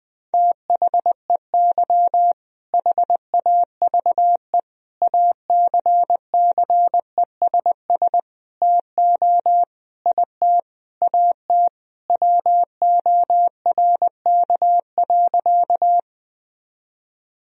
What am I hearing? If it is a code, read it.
Morse code, heard as a keyed tone, about 20 wpm, 700 Hz. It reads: THEY HAVE ACCESS TO IT AT WORK.
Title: Sentences from Top 1000 Words - Review Part 4 - 20wpm